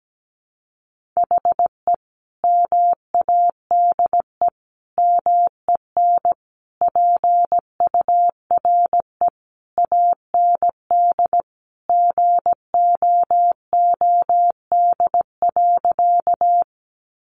Morse code, heard as a keyed tone, about 17 wpm, 700 Hz. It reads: HE MADE MEN PURE AND GOOD.
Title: Sentences from Top 1500 Words 17wpm